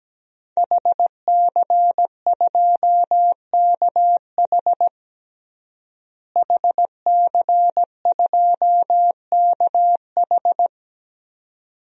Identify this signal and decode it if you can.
Morse code, heard as a keyed tone, about 17 wpm, 700 Hz. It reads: HC2KH HC2KH
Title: DXCC-100 Call Signs - 5 Characters - Repeated 2x - 17wpm